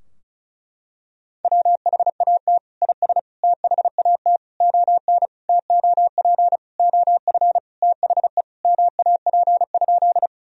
Morse code, heard as a keyed tone, about 36 wpm, 700 Hz. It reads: WHAT IS THAT ON TOP OF THE MAP?
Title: Sentences from Top 1000 Words - Review Part 2 - 35wpm